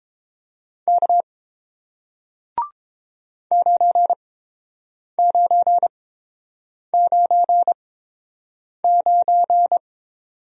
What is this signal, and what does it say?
Morse code, heard as a keyed tone, about 30 wpm, spaced as for 7 wpm, 700 Hz. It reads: K 9999